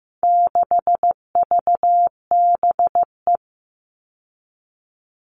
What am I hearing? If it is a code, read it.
Morse code, heard as a keyed tone, about 15 wpm, 700 Hz. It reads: 6VBE
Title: All Cut Numbers - 4-Digit 15wpm